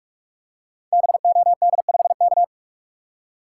Morse code, heard as a keyed tone, about 44 wpm, 700 Hz. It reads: BOD5K
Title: Code Groups - 5 Letter:Numbers - 45wpm